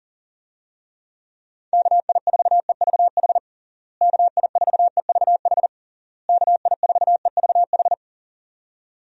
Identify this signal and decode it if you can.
Morse code, heard as a keyed tone, about 37 wpm, 700 Hz. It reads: KI4EVH KI4EVH KI4EVH
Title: USA Call Signs - 6 Characters - Repeated 3x - 40wpm